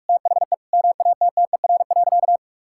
Morse code, heard as a keyed tone, about 46 wpm, 700 Hz. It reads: THE MATTER.
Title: Sentences from Top 1000 Words - Review Part 6 - 45wpm